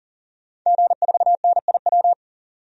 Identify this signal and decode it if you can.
Morse code, heard as a keyed tone, about 40 wpm, 700 Hz. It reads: G4NIW
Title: DXCC-100 Call Signs - 5 Characters - 40wpm